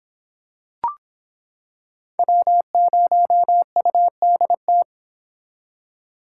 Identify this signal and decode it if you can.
Morse code, heard as a keyed tone, about 26 wpm, 700 Hz. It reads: W0UDT